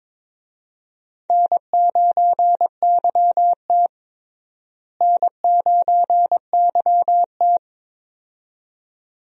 Morse code, heard as a keyed tone, about 22 wpm, 700 Hz. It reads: N9YT N9YT
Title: USA Call Signs - 4 Characters - Repeated 2x - 22wpm